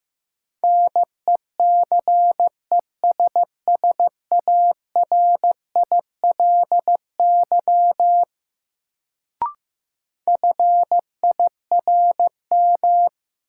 Morse code, heard as a keyed tone, about 15 wpm, 700 Hz. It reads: NECESSARILY FIRM
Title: Sentences from Top 1400 Words - New Word Drill Alt - 15wpm